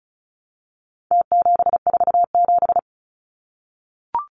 Morse code, heard as a keyed tone, about 34 wpm, 700 Hz. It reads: T747
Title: Numbers - Contest Style - 4-Digit 35wpm